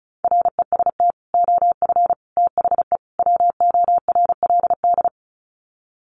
Morse code, heard as a keyed tone, about 34 wpm, 700 Hz. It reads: REST OF THE WORLD
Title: Sets of 4 Words 35wpm